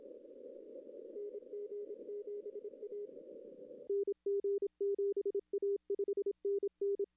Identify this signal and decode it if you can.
Morse code, heard as a keyed tone, about 27 wpm, 400 Hz. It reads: NG7A NG7A 5NN
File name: contests / arrl-160-k5tr-op-w5tn -0200